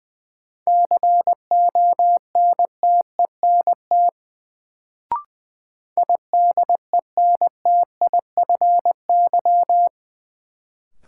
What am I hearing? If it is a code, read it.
Morse code, heard as a keyed tone, about 20 wpm, 700 Hz. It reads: CONTENT IDENTIFY
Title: Sentences from Top 900 Words - New Word Drill - 20wpm